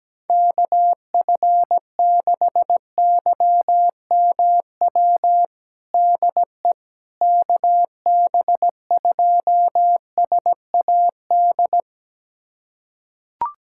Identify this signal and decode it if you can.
Morse code, heard as a keyed tone, about 17 wpm, 700 Hz. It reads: KF6YMW DE KB2SAD
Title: USA Callsign DE Callsign 17wpm